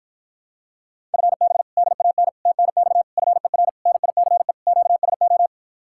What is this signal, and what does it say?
Morse code, heard as a keyed tone, about 47 wpm, 700 Hz. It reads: FB DAN TNX FER NICE QSO